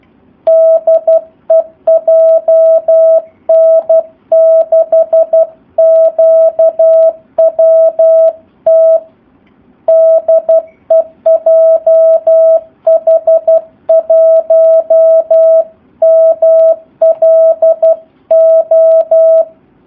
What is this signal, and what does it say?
Morse code, heard as a keyed tone, about 12 wpm, 650 Hz. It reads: DE JN6QWT DE JH1MLO
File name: jh8cbh / 2019cw / SP04- w